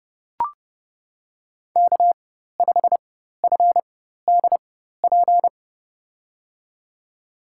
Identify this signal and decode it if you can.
Morse code, heard as a keyed tone, about 30 wpm, spaced as for 14 wpm, 700 Hz. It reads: K5FDP
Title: USA Call Signs - 5 Characters - 4x Character Spacing - 30wpm